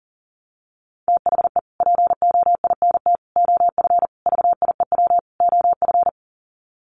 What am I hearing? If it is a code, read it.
Morse code, heard as a keyed tone, about 40 wpm, 700 Hz. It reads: THE POINT OF VIEW OF
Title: Sets of 5 Words 40wpm